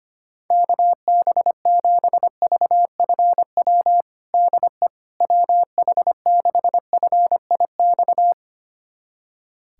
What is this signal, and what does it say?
Morse code, heard as a keyed tone, about 25 wpm, 700 Hz. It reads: KB7VFW DE WH6FIX
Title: USA Callsign DE Callsign - 6 Characters - 25wpm